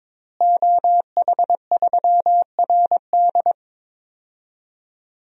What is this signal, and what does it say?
Morse code, heard as a keyed tone, about 22 wpm, 700 Hz. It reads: OH3RD
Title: DXCC-100 Call Signs - 5 Characters - Repeated 3x - 22wpm